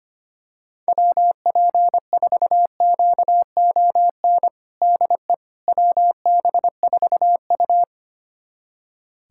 Morse code, heard as a keyed tone, about 25 wpm, 700 Hz. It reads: WP4QON DE WB4U
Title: USA Callsign DE Callsign 25wpm